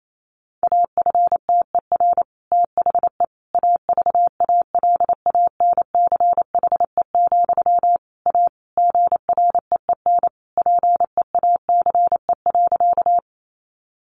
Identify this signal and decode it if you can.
Morse code, heard as a keyed tone, about 28 wpm, 700 Hz. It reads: AFTER THE AVALANCHE, A GREEN PEACE.
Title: A Sound Of Thunder 28wpm